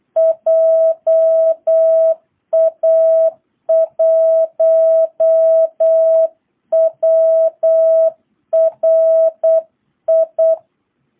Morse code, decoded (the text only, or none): JA1WRI